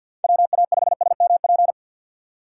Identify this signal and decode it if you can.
Morse code, heard as a keyed tone, about 53 wpm, 700 Hz. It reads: WA5SMP
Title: USA Call Signs - 6 Characters - 3x Character Spacing - 50wpm